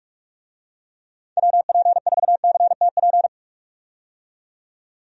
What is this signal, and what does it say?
Morse code, heard as a keyed tone, about 41 wpm, 700 Hz. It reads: WP4CTP
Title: USA Call Signs - 6 Characters - 45wpm